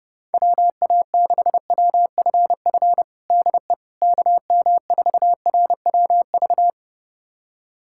Morse code, heard as a keyed tone, about 30 wpm, 700 Hz. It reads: WA6WFF DE KM4RWV
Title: USA Callsign DE Callsign 30wpm